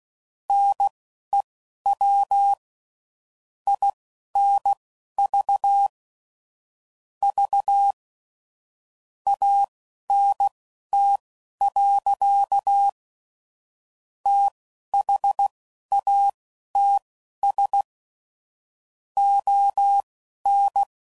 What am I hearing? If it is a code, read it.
Morse code, heard as a keyed tone, about 16 wpm, spaced as for 10 wpm, 800 Hz. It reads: NEW INVVANT. THATS ON
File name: CW Code Groups @15WPM / QSOs / WD0FBT Spaced